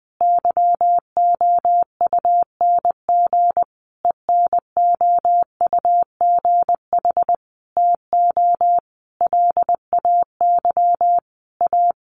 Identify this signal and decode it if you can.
Morse code, heard as a keyed tone, about 20 wpm, 700 Hz. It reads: YOUNG ENOUGH TO LAY A
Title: Sentences from Top 1200 Words 20wpm